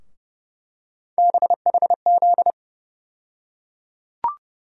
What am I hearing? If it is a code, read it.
Morse code, heard as a keyed tone, about 30 wpm, 700 Hz. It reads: BHZ
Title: USA Call Sign Postfixes - 3 Letters 30wpm